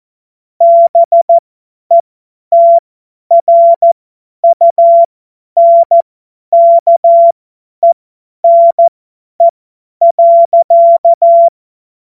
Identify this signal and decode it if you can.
Morse code, heard as a keyed tone, about 14 wpm, spaced as for 10 wpm, 680 Hz. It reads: BETRUNKENE.
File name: Morsekurs / Witze / Witz 01. Juni 2021, 50 F70